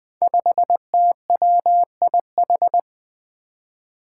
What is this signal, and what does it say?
Morse code, heard as a keyed tone, about 20 wpm, 700 Hz. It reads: 5TWIH
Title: Five Challenging Letter-Number Combinations 20wpm